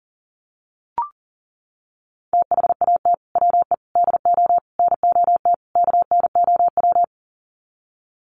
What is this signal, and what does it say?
Morse code, heard as a keyed tone, about 40 wpm, 700 Hz. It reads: THAT WE DO NOT KNOW
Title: Sets of 5 Words Encore2 40wpm